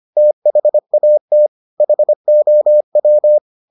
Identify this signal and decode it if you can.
Morse code, heard as a keyed tone, about 25 wpm, 600 Hz. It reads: THAT HOW